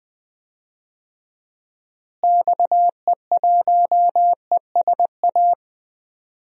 Morse code, heard as a keyed tone, about 21 wpm, 700 Hz. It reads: XE1ESA
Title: DXCC-100 Call Signs - 6 Characters - Repeated 2x - 20wpm